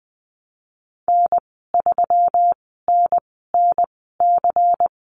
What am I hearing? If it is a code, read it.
Morse code, heard as a keyed tone, about 19 wpm, spaced as for 14 wpm, 700 Hz. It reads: N3NNC
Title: 122 Lesson 20g - Number 3 - Call Signs 20wpm